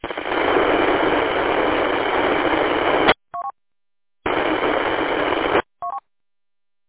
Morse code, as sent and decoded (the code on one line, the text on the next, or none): none